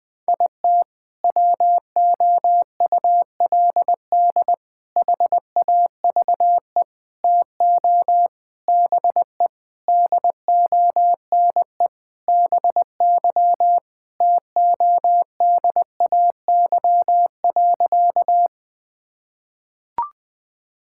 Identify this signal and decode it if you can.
Morse code, heard as a keyed tone, about 20 wpm, 700 Hz. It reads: IT WOULD HAVE TO BE DONE BY TODAY.
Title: Sentences with 4-Word Phrases - 20wpm